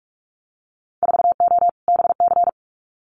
Morse code, heard as a keyed tone, about 46 wpm, 690 Hz. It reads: 4O BC
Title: Winter Field Day Exchanges Encore 45wpm